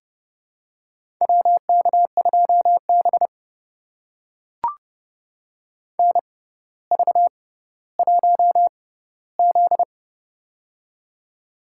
Morse code, heard as a keyed tone, about 30 wpm, spaced as for 15 wpm, 700 Hz. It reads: WK2B NV1Z
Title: USA Call Signs - 4 Characters - 6x Character Spacing - 30wpm